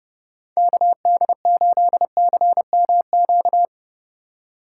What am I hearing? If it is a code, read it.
Morse code, heard as a keyed tone, about 40 wpm, 700 Hz. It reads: KD8CMQ